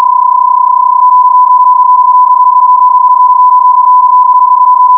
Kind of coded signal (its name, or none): PSK31